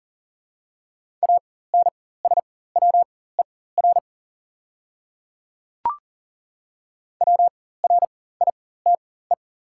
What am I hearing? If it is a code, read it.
Morse code, heard as a keyed tone, about 40 wpm, spaced as for 19 wpm, 690 Hz. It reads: ANSWER WRITE
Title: 4x Character Spacing - Top 300 Words - 40wpm